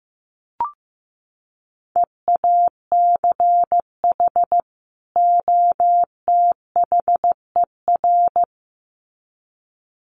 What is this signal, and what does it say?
Morse code, heard as a keyed tone, about 15 wpm, 700 Hz. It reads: EACH OTHER
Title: Sets of 2 Words 15wpm